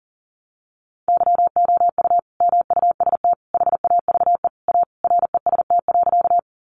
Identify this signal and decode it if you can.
Morse code, heard as a keyed tone, about 40 wpm, 700 Hz. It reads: YOU MUST HAVE A REST.
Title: Sentences from Top 500 Words 40wpm